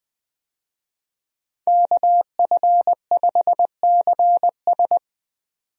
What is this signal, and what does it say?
Morse code, heard as a keyed tone, about 20 wpm, 700 Hz. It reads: KF5CS
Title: USA Call Signs - 5 Characters - 20wpm